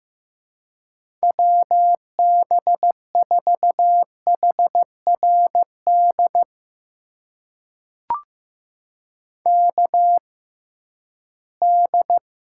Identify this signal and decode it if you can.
Morse code, heard as a keyed tone, about 15 wpm, spaced as for 11 wpm, 700 Hz. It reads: WB4HRD KD
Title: USA Call Signs - 6 Characters - 6x Character Spacing - 15wpm